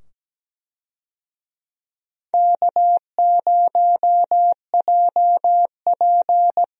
Morse code, heard as a keyed tone, about 17 wpm, 700 Hz. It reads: K0JP
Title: USA Call Signs - 4 Characters - 17wpm